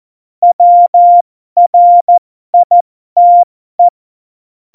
Morse code, heard as a keyed tone, about 14 wpm, spaced as for 12 wpm, 700 Hz. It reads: WRITE
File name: Morsekurs / Witze